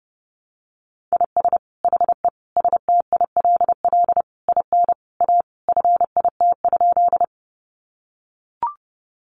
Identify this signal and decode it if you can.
Morse code, heard as a keyed tone, about 30 wpm, 700 Hz. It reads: IS HE STILL IN A FIT?
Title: Sentences from Top 800 Words 30wpm